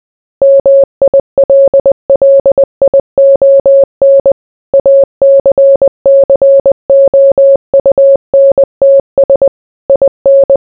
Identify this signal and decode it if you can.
Morse code, heard as a keyed tone, about 20 wpm, 550 Hz. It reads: MILLION ACCOUNTS IN